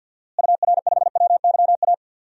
Morse code, heard as a keyed tone, about 52 wpm, 700 Hz. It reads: UR5WYA
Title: DXCC-100 Call Signs - 6 Characters - Repeated 2x - 50wpm